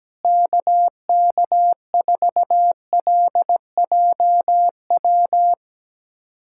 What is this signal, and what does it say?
Morse code, heard as a keyed tone, about 17 wpm, 700 Hz. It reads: KK4LJW
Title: USA Callsign DE Callsign 17wpm